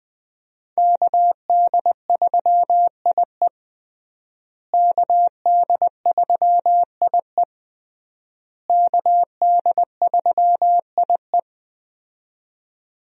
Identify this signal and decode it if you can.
Morse code, heard as a keyed tone, about 21 wpm, 700 Hz. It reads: KD3IE KD3IE KD3IE
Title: USA Call Signs - 5 Characters - Repeated 3x - 20wpm